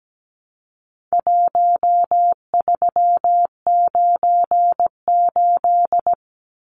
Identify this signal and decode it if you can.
Morse code, heard as a keyed tone, about 17 wpm, 700 Hz. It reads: 1398
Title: Numbers - Contest Style - 4-Digit 17wpm